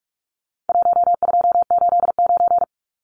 Morse code, heard as a keyed tone, about 44 wpm, 700 Hz. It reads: 1289
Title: Numbers - Contest Style - 4-Digit 45wpm